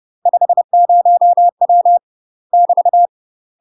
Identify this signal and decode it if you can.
Morse code, heard as a keyed tone, about 30 wpm, 700 Hz. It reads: 50W <BT>
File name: UK MASTER QSOs TRAINING / 30WPM / UK MASTER QSOs 0040